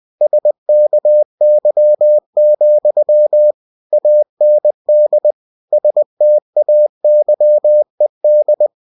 Morse code, heard as a keyed tone, about 20 wpm, 600 Hz. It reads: SKY, AND STAYED
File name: cw-books / winnie-the-pooh / winnie-20wpm-page-0006